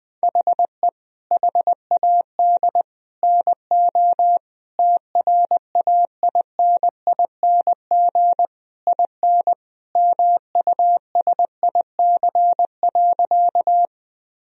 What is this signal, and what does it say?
Morse code, heard as a keyed tone, about 21 wpm, 700 Hz. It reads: HE HAD NO TRAINING IN MUSIC.
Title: Sentences from Top 1000 Words - Review Part 4 - 20wpm